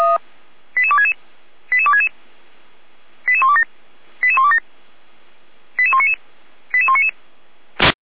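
Five-tone selective call (ZVEI), heard as quick runs of five tones, tones of about 70 ms. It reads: 88266 88266 88127 88127 88199 88199
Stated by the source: ZVEI five-tone selective call received on an SDR, recorded by fire brigade Feuerwehr Orlamünde.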